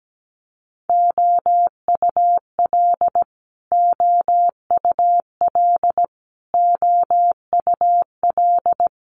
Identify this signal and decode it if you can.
Morse code, heard as a keyed tone, about 17 wpm, 700 Hz. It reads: OUL OUL OUL
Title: Multi-letter Phonemes Repeated 3x 17wpm